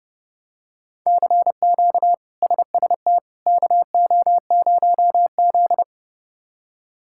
Morse code, heard as a keyed tone, about 30 wpm, 700 Hz. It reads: CQ SST KO0Z
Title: K1USN SST QSO Practice - 30wpm